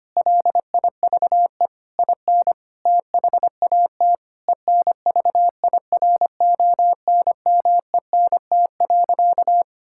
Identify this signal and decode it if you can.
Morse code, heard as a keyed tone, about 25 wpm, 700 Hz. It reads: LIVE IN THAT ENVIRONMENT.
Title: Sentences from Top 1000 Words - Review Part 5 - 25wpm